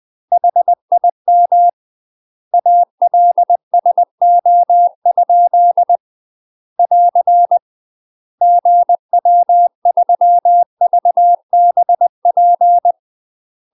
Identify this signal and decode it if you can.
Morse code, heard as a keyed tone, about 20 wpm, 700 Hz. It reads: HIM ALSO? <AR> GW3VBP
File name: UK MASTER QSOs TRAINING / 20WPM / UK MASTER QSOs 0023